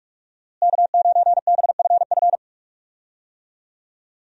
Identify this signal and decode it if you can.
Morse code, heard as a keyed tone, about 46 wpm, 700 Hz. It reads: K9BFF
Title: USA Call Signs - 5 Characters - Repeated 2x - 45wpm